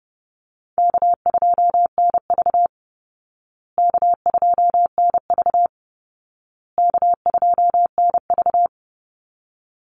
Morse code, heard as a keyed tone, about 30 wpm, 700 Hz. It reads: K2NV K2NV K2NV
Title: Call Signs Repeated 2x - 30wpm